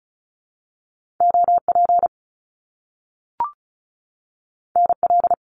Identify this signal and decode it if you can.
Morse code, heard as a keyed tone, about 36 wpm, 700 Hz. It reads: OP NL
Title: Top Two Letter Combinations in Words 35wpm